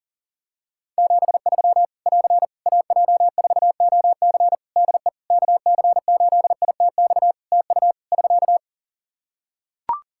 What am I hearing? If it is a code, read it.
Morse code, heard as a keyed tone, about 37 wpm, 700 Hz. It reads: 73 <AR> AJ4OC DE KC8ITX TU <SK>
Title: Practice QSO - Exchange 4 - Ending - 40wpm